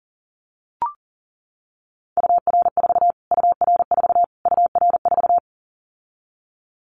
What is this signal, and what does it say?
Morse code, heard as a keyed tone, about 40 wpm, 700 Hz. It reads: UR4 UR4 UR4
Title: Callsign Prefixes for Top 25 DXCC - Repeated 3x - 40wpm